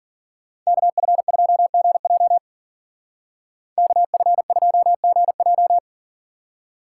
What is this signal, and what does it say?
Morse code, heard as a keyed tone, about 50 wpm, 700 Hz. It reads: KF2GJ KF2GJ
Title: Speed Racing - USA Call Signs 40wpm